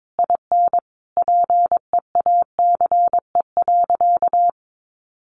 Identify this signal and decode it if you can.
Morse code, heard as a keyed tone, about 22 wpm, 700 Hz. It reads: IN PEACE.
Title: Sentences from Top 1500 Words 22wpm